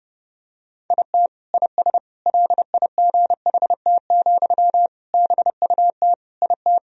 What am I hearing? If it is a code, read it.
Morse code, heard as a keyed tone, about 32 wpm, 700 Hz. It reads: IT IS LIGHT, BUT IT
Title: Sentences from Top 1000 Words - Review Part 1 - 30wpm